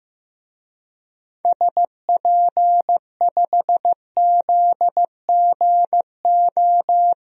Morse code, heard as a keyed tone, about 15 wpm, 700 Hz. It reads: SP5ZGO
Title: DXCC-100 Call Signs - 6 Characters - 15wpm